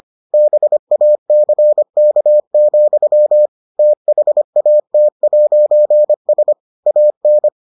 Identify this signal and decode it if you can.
Morse code, heard as a keyed tone, about 25 wpm, 600 Hz. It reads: BACK, THAT'S AN